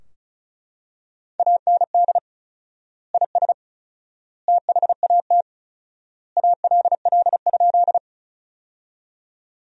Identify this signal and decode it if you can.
Morse code, heard as a keyed tone, about 34 wpm, 700 Hz. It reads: AND IS THAT ALL?
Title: 4x Word-Spacing - Sentences from Top 100 Words - 35wpm